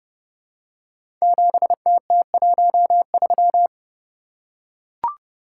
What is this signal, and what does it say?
Morse code, heard as a keyed tone, about 30 wpm, 700 Hz. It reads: 7TT13
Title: Code Groups - 5 Letter:Numbers - 30wpm